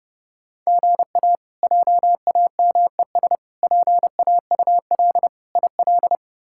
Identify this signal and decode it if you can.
Morse code, heard as a keyed tone, about 30 wpm, 700 Hz. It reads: GA JAMES PAUL IL
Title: K1USN SST QSO Practice - 30wpm